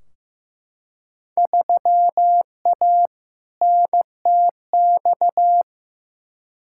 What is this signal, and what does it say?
Morse code, heard as a keyed tone, about 15 wpm, 700 Hz. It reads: 3A NTX
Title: Field Day Exchanges - 15wpm